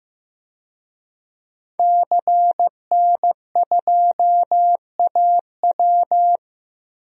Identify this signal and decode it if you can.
Morse code, heard as a keyed tone, about 15 wpm, 700 Hz. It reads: CN2AW
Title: DXCC-100 Call Signs - 5 Characters - Repeated 2x - 15wpm